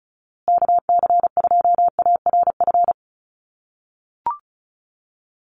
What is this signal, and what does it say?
Morse code, heard as a keyed tone, about 34 wpm, 700 Hz. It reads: KC2ARF
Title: USA Call Signs Repeated 2x 35wpm